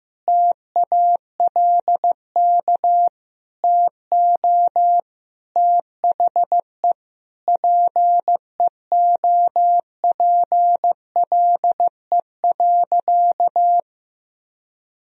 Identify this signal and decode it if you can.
Morse code, heard as a keyed tone, about 15 wpm, 700 Hz. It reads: TALK TO THE PEOPLE.
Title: Sentences from Top 400 Words and Abbreviations 15wpm